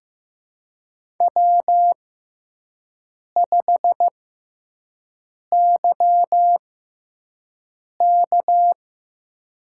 Morse code, heard as a keyed tone, about 15 wpm, spaced as for 5 wpm, 700 Hz. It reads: W5YK
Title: USA Call Signs - 6x Character Spacing - 15wpm